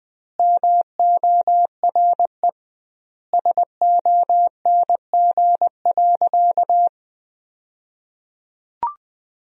Morse code, heard as a keyed tone, about 20 wpm, 700 Hz. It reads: MORE SONG.
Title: 2x Word-Spacing - Sentences from Top 400 Words - 20wpm